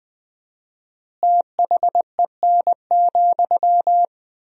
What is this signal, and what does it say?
Morse code, heard as a keyed tone, about 20 wpm, 700 Hz. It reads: THEN,
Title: 4x Word-Spacing - Sentences from Top 100 Words - 20wpm